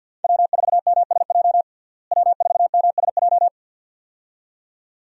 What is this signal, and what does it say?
Morse code, heard as a keyed tone, about 52 wpm, 700 Hz. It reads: W4MSJ W4MSJ
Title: USA Call Signs - 5 Characters - Repeated 2x - 50wpm